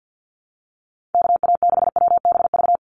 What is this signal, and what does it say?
Morse code, heard as a keyed tone, about 49 wpm, 700 Hz. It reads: KA6WBV
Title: USA Call Signs 50wpm